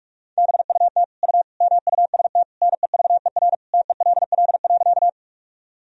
Morse code, heard as a keyed tone, about 46 wpm, 700 Hz. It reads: BUT U MUST NEVER TELL.
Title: Sentences from Top 300 Words and Abbreviations 45wpm